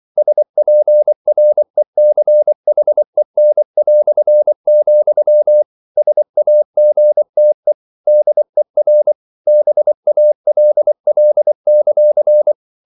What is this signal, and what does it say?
Morse code, heard as a keyed tone, about 24 wpm, 600 Hz. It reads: SPRECHEN", SAGTE DER BALL;